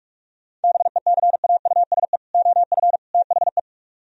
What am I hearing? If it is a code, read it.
Morse code, heard as a keyed tone, about 49 wpm, 700 Hz. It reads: BECAUSE OF THE